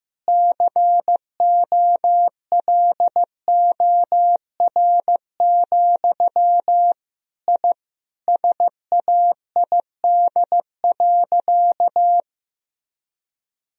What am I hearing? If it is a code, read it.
Morse code, heard as a keyed tone, about 15 wpm, 700 Hz. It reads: COLOR, I SAID.
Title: Sentences from Top 1000 Words - Review Part 2 - 15wpm